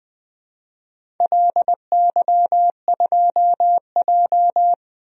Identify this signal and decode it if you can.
Morse code, heard as a keyed tone, about 21 wpm, 700 Hz. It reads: LY2J
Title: DXCC-100 Call Signs - 4 Characters - 20wpm